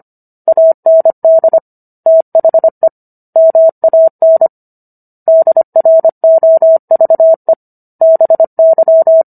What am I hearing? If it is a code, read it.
Morse code, heard as a keyed tone, about 25 wpm, 650 Hz. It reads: AND THE MAN DROVE BY